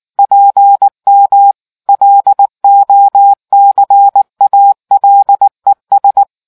Morse code, heard as a keyed tone, about 19 wpm, 800 Hz. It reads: PM LOCALES